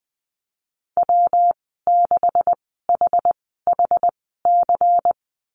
Morse code, heard as a keyed tone, about 20 wpm, spaced as for 15 wpm, 700 Hz. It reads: W6HHC
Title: 161 Lesson 26g - Letter g - Call Signs 20wpm